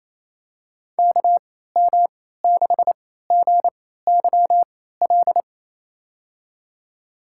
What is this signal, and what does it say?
Morse code, heard as a keyed tone, about 28 wpm, spaced as for 16 wpm, 700 Hz. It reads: KM6GYL
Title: USA Call Signs - 6 Characters - 3x Character Spacing - 28wpm